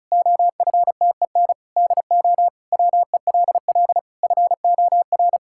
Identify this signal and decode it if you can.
Morse code, heard as a keyed tone, about 36 wpm, 700 Hz. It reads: OFTEN DO WELL FOR
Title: Sentences from Top 1000 Words - Review Part 5 - 35wpm